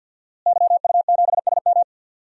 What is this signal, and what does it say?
Morse code, heard as a keyed tone, about 47 wpm, 700 Hz. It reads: YU7SM
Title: DXCC-100 Call Signs - 5 Characters - Repeated 3x - 50wpm